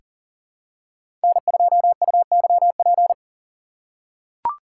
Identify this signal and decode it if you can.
Morse code, heard as a keyed tone, about 40 wpm, 700 Hz. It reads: N2UYP
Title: USA Call Signs - 5 Characters - 2x Character Spacing - 40wpm